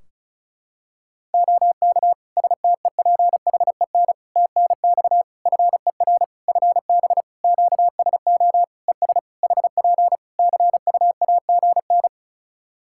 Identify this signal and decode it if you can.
Morse code, heard as a keyed tone, about 34 wpm, 700 Hz. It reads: OK STEPHEN TNX FER FB QSO ES HP CUAGN
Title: Practice QSO - Exchange 4 - Ending - 35wpm